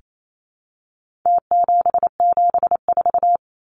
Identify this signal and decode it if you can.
Morse code, heard as a keyed tone, about 28 wpm, 700 Hz. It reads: T774
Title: Numbers - Contest Style - 4-Digit 28wpm